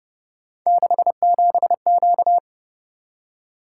Morse code, heard as a keyed tone, about 30 wpm, 700 Hz. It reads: 67Q